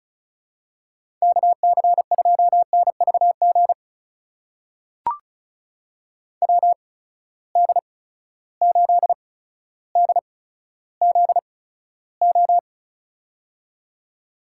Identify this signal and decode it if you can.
Morse code, heard as a keyed tone, about 36 wpm, spaced as for 15 wpm, 700 Hz. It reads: KC2NVG WD8DZO
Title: USA Call Signs - 6 Characters - 8x Character Spacing - 35wpm